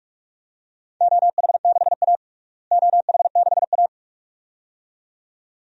Morse code, heard as a keyed tone, about 46 wpm, 700 Hz. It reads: OH6A OH6A
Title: DXCC-100 Call Signs - 4 Characters - Repeated 2x - 45wpm